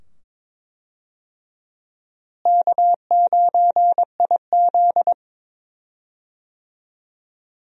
Morse code, heard as a keyed tone, about 22 wpm, 700 Hz. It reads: K9IZ
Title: USA Call Signs - 4 Characters - Repeated 2x - 22wpm